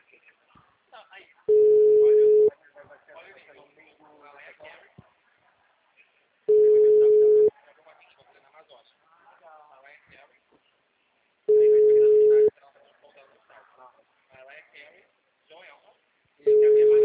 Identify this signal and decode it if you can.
Ringback tone (Europe, 425 Hz), with speech.